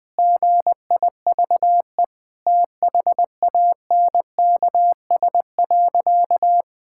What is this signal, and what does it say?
Morse code, heard as a keyed tone, about 20 wpm, 700 Hz. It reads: GIVE THANKS.
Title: Sentences with 2-Word Phrases - 20wpm